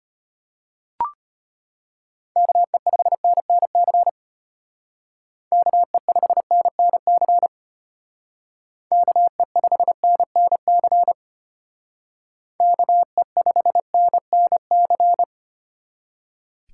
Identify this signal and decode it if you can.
Morse code, heard as a keyed tone, about 33 wpm, 700 Hz. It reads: KE5NNC KE5NNC KE5NNC KE5NNC